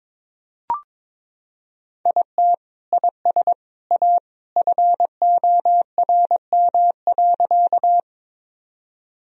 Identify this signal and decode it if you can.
Morse code, heard as a keyed tone, about 22 wpm, 700 Hz. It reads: IT IS A FORM.
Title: Sentences from Top 1000 Words - Review Part 1 - 22wpm